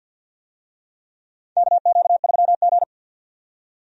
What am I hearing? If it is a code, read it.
Morse code, heard as a keyed tone, about 52 wpm, 700 Hz. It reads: KQ3G
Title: USA Call Signs - 4 Characters - 50wpm